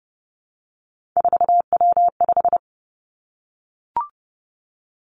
Morse code, heard as a keyed tone, about 30 wpm, 700 Hz. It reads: 4W5
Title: Three Challenging Letter-Number Combinations 30wpm